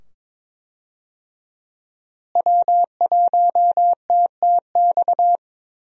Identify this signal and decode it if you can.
Morse code, heard as a keyed tone, about 22 wpm, 700 Hz. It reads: W1TTX
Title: USA Call Signs - 5 Characters - Repeated 2x - 22wpm